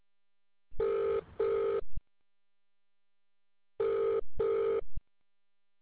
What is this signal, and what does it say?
Ringback tone (UK), 400 + 450 Hz.